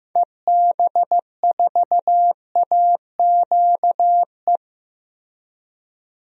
Morse code, heard as a keyed tone, about 15 wpm, 700 Hz. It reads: EB4AQE
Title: DXCC-100 Call Signs - 6 Characters - Repeated 2x - 15wpm